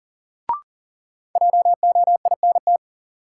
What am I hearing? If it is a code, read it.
Morse code, heard as a keyed tone, about 40 wpm, 700 Hz. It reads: JOINT
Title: Sentences from Top 1300 Words - New Word Drill - 40wpm